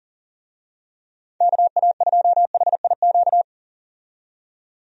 Morse code, heard as a keyed tone, about 40 wpm, 700 Hz. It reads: KA2HIQ
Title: USA Call Signs - 6 Characters - 40wpm